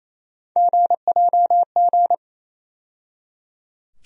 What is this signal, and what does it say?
Morse code, heard as a keyed tone, about 28 wpm, 700 Hz. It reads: GJG
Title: USA Call Sign Postfixes - 3 Letters 28wpm